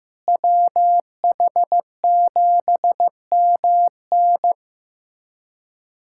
Morse code, heard as a keyed tone, about 15 wpm, 700 Hz. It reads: WH7MN
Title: USA Callsign DE Callsign - 5 Characters - 15wpm